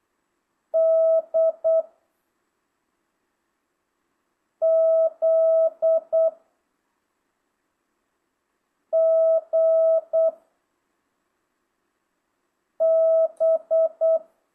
Morse code, heard as a keyed tone, about 8 wpm, 660 Hz. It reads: D Z G B
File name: jh8cbh / 2025cw